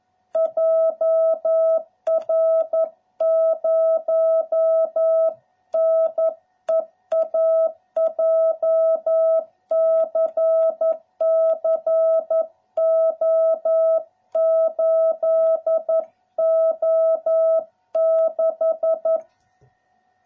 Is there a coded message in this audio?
Yes: Morse code from 0.4 to 19.2 s, heard as a keyed tone, about 11 wpm, 660 Hz. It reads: JR0NEA JCC O8O6